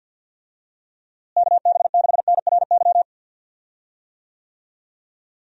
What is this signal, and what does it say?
Morse code, heard as a keyed tone, about 52 wpm, 700 Hz. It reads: KB6NRY
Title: USA Call Signs - 6 Characters - Repeated 3x - 50wpm